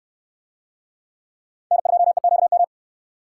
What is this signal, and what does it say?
Morse code, heard as a keyed tone, about 69 wpm, 690 Hz. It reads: N2EQM